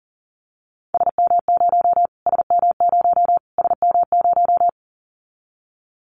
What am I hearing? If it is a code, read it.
Morse code, heard as a keyed tone, about 40 wpm, 700 Hz. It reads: SM0 SM0 SM0
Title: Callsign Prefixes for Top 25 DXCC - Repeated 3x - 40wpm